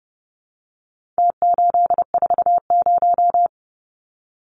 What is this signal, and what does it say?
Morse code, heard as a keyed tone, about 30 wpm, 700 Hz. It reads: T840
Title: Numbers - Contest Style - 4-Digit 30wpm